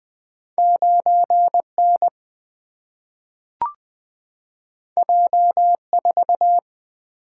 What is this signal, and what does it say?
Morse code, heard as a keyed tone, about 20 wpm, 700 Hz. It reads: 9 N J 4
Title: Single-Letter Single-Number 20wpm